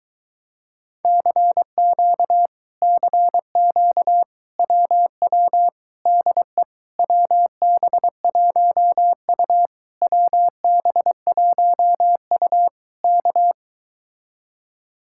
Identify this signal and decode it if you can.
Morse code, heard as a keyed tone, about 23 wpm, 700 Hz. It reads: CQ CQ WW DE WB1U WB1U K